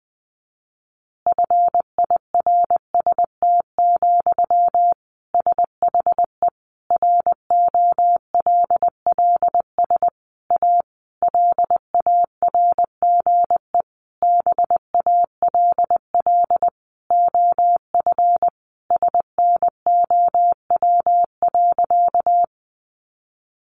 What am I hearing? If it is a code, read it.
Morse code, heard as a keyed tone, about 19 wpm, 700 Hz. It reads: FIRST, SHE ROLLS A LARGE BALL OF SNOW.
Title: Easy Phrases 20wpm